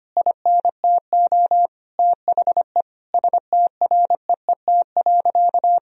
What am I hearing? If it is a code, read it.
Morse code, heard as a keyed tone, about 25 wpm, 700 Hz. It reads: INTO THE STREET.
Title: Sentences from Top 1000 Words - Review Part 4 - 25wpm